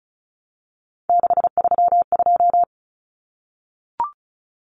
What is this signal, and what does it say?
Morse code, heard as a keyed tone, about 34 wpm, 700 Hz. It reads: 632
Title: Common Cut Numbers - 3-Digit 35wpm